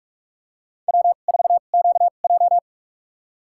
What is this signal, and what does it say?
Morse code, heard as a keyed tone, about 41 wpm, spaced as for 31 wpm, 700 Hz. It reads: W4QJ
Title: USA Call Signs - 2x Character Spacing - 45wpm